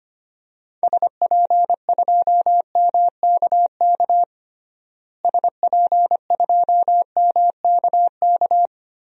Morse code, heard as a keyed tone, about 25 wpm, 700 Hz. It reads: SP2MKK SP2MKK
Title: DXCC-100 Call Signs - 6 Characters - Repeated 3x - 25wpm